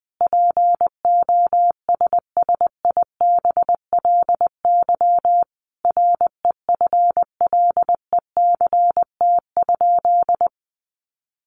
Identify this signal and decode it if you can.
Morse code, heard as a keyed tone, about 20 wpm, 700 Hz. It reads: POSSIBLY REFLECT?
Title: Sentences from Top 1500 Words 20wpm